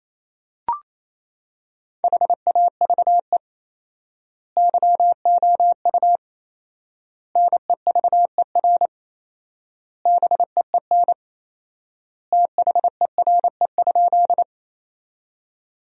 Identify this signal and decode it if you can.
Morse code, heard as a keyed tone, about 28 wpm, 700 Hz. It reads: HAVE YOU NEVER BEEN THERE?
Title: 4x Word-Spacing - Sentences from Top 300 Words - 28wpm